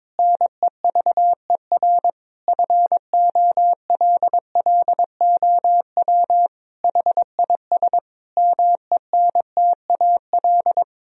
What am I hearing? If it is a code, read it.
Morse code, heard as a keyed tone, about 22 wpm, 700 Hz. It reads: NEVER FOLLOW HIS MENTAL
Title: Sentences from Top 1000 Words - Review Part 4 - 22wpm